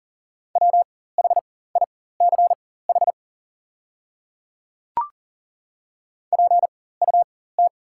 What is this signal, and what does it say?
Morse code, heard as a keyed tone, about 40 wpm, spaced as for 19 wpm, 700 Hz. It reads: WHICH PUT
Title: 4x Character Spacing - Top 300 Words - 40wpm